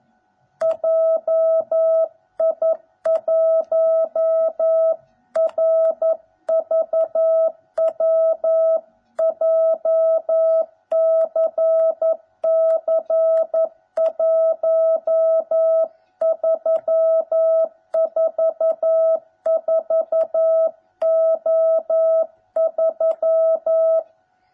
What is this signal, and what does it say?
Morse code, heard as a keyed tone, about 11 wpm, 650 Hz. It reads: JI1RVW JCC 1344O3